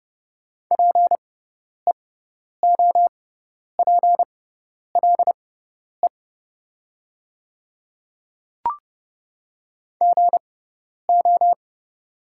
Morse code, heard as a keyed tone, about 30 wpm, spaced as for 10 wpm, 690 Hz. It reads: PEOPLE GO
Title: 6x Character Spacing - Top 200 Words - 30wpm